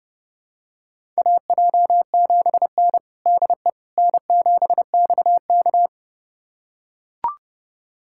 Morse code, heard as a keyed tone, about 32 wpm, 700 Hz. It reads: AJ7N DE N7XK
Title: USA Callsign DE Callsign - 4 Characters - 30wpm